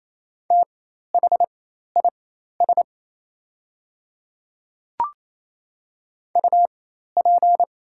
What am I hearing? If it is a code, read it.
Morse code, heard as a keyed tone, about 28 wpm, spaced as for 13 wpm, 700 Hz. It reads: THIS UP
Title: 4x Character Spacing - Top 300 Words - 28wpm